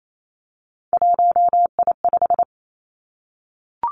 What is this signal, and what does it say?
Morse code, heard as a keyed tone, about 28 wpm, 700 Hz. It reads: 1I5
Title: Three Challenging Letter-Number Combinations 28wpm